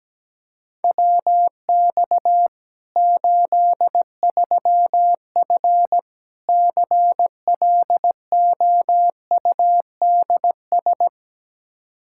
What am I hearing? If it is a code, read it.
Morse code, heard as a keyed tone, about 17 wpm, 700 Hz. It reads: WX 83F CLOUDS
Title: Practice QSO - Exchange 2 - Info - 17wpm